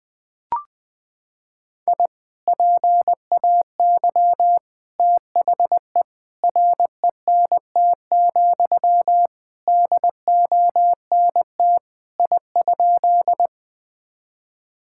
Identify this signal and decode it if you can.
Morse code, heard as a keyed tone, about 20 wpm, 700 Hz. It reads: I PAY THE RENT, DONT I?
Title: Sentences from Top 1000 Words - Review Part 5 - 20wpm